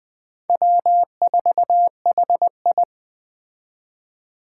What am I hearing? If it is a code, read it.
Morse code, heard as a keyed tone, about 20 wpm, 700 Hz. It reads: W4HI